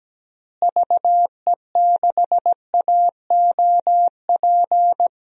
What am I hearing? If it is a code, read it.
Morse code, heard as a keyed tone, about 17 wpm, 700 Hz. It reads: VE6AOP
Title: DXCC-100 Call Signs - 6 Characters - Repeated 2x - 17wpm